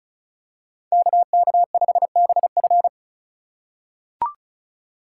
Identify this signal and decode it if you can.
Morse code, heard as a keyed tone, about 34 wpm, 700 Hz. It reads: KK5BF
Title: USA Call Signs - 5 Characters - 2x Character Spacing - 35wpm